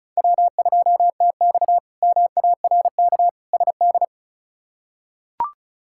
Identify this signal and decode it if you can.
Morse code, heard as a keyed tone, about 36 wpm, 700 Hz. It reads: W2TX MARK SD
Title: K1USN SST QSO Practice - 35wpm